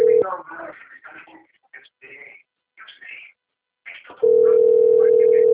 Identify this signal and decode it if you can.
Ringback tone (North America), 440 + 480 Hz, with speech.
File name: out- - - - - .6953